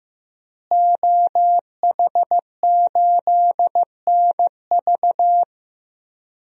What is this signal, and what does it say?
Morse code, heard as a keyed tone, about 16 wpm, 700 Hz. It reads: OH8NV
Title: DXCC-100 Call Signs - 5 Characters - Repeated 3x - 15wpm